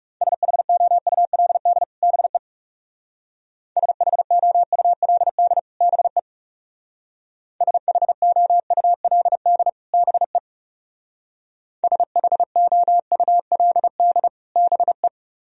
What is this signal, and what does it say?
Morse code, heard as a keyed tone, about 34 wpm, 700 Hz. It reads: SHOULD BE SHOULD BE SHOULD BE SHOULD BE
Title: Speed-Racing - Sets of 2 Words 30wpm